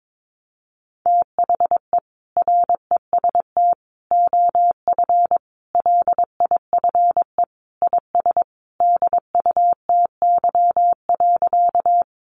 Morse code, heard as a keyed tone, about 22 wpm, 700 Hz. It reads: THE REST OF LIFE IS DUTY.
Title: Sentences from Top 1500 Words 22wpm